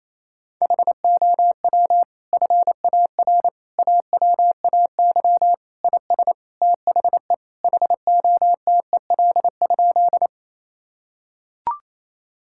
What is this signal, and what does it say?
Morse code, heard as a keyed tone, about 28 wpm, 700 Hz. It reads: HOW FAR AWAY IS THE HOTEL?
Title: Sentences from Top 1600 Words 28wpm